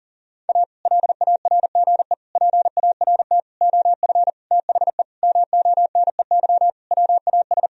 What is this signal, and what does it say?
Morse code, heard as a keyed tone, about 40 wpm, 700 Hz. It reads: A LARGE PART OF THE MONEY WAS
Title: Sentences from Top 1000 Words - Review Part 1 - 40wpm